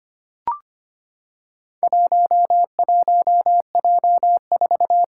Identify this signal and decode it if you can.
Morse code, heard as a keyed tone, about 25 wpm, 700 Hz. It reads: 11J4
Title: Four Challenging Letter-Number Combinations 25wpm